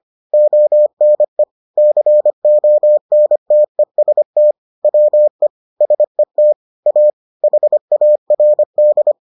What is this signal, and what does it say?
Morse code, heard as a keyed tone, about 25 wpm, 600 Hz. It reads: ONE CONTEST WE SET A HARD